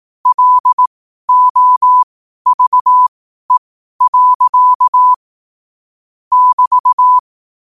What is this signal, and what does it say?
Morse code, heard as a keyed tone, about 18 wpm, spaced as for 12 wpm, 1000 Hz. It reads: LOVE. <BT>
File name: qotdfiles / 12wpm